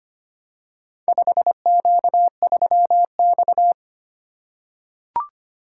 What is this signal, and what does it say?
Morse code, heard as a keyed tone, about 25 wpm, 700 Hz. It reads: 5Q3X
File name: practice / 25wpm / Four Letter-Number Combinations - 25wpm